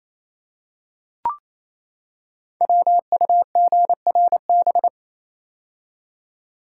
Morse code, heard as a keyed tone, about 27 wpm, 700 Hz. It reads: WUGRB
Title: Code Groups - 5 Letters - 28wpm